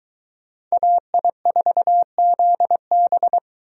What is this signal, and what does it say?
Morse code, heard as a keyed tone, about 22 wpm, 700 Hz. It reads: AI4ZB